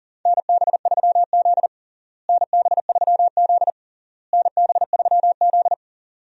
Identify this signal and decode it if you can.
Morse code, heard as a keyed tone, about 40 wpm, 700 Hz. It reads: NB3Z NB3Z NB3Z
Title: USA Call Signs - 4 Characters - Repeated 3x - 40wpm